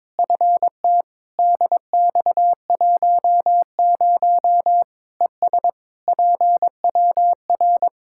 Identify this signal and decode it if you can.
Morse code, heard as a keyed tone, about 22 wpm, 700 Hz. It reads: FT DX10 ES PWR
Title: Practice QSO - Exchange 2 - Info - 22wpm